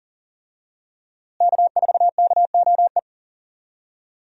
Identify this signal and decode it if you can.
Morse code, heard as a keyed tone, about 40 wpm, 700 Hz. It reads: K4KOE
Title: USA Call Signs - 5 Characters - Repeated 3x - 40wpm